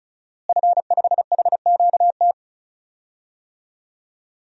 Morse code, heard as a keyed tone, about 36 wpm, 700 Hz. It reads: F5HQT
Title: DXCC-100 Call Signs - 5 Characters - Repeated 2x - 35wpm